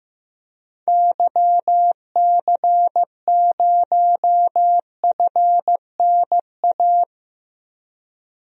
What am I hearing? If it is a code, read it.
Morse code, heard as a keyed tone, about 15 wpm, 700 Hz. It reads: YC0FNA
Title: DXCC-100 Call Signs - 6 Characters - 15wpm